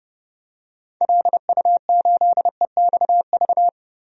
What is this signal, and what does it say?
Morse code, heard as a keyed tone, about 31 wpm, 700 Hz. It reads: LU8EXV
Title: DXCC-100 Call Signs - 6 Characters - 30wpm